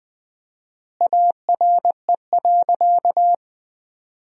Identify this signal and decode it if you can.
Morse code, heard as a keyed tone, about 20 wpm, 700 Hz. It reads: ARE.
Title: 4x Word-Spacing - Sentences from Top 500 Words - 20wpm